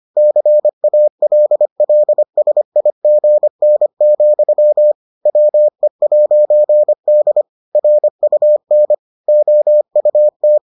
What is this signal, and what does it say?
Morse code, heard as a keyed tone, about 25 wpm, 600 Hz. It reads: CALLSIGN, WE'D RUN OUT